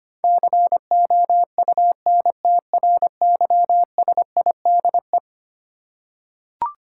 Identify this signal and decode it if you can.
Morse code, heard as a keyed tone, about 25 wpm, 700 Hz. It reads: COUNTRYSIDE
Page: morse-code-ninja-practice-25wpm